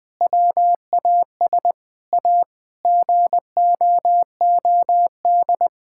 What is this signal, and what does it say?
Morse code, heard as a keyed tone, about 21 wpm, 700 Hz. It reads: WAS A GOOD
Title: Sentences with 4-Word Phrases - 20wpm